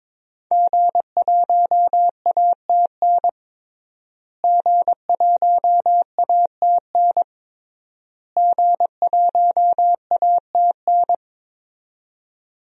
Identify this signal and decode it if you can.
Morse code, heard as a keyed tone, about 22 wpm, 700 Hz. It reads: G1ATN G1ATN G1ATN
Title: DXCC-100 Call Signs - 5 Characters - Repeated 3x - 22wpm